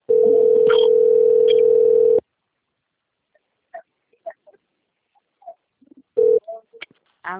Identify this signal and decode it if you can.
Ringback tone (North America), 440 + 480 Hz.